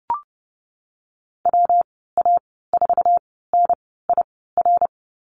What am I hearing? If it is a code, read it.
Morse code, heard as a keyed tone, about 31 wpm, 700 Hz. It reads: W A 4 N I R
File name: Lesson 19g - Letter w - Call Signs 30wpm